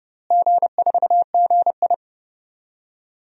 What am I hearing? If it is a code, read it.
Morse code, heard as a keyed tone, about 32 wpm, 700 Hz. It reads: G4GI
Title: DXCC-100 Call Signs - 4 Characters - Repeated 3x - 30wpm